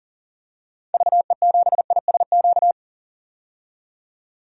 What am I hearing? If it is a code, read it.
Morse code, heard as a keyed tone, about 37 wpm, 700 Hz. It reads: VE7ISQ
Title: DXCC-100 Call Signs - 6 Characters - 40wpm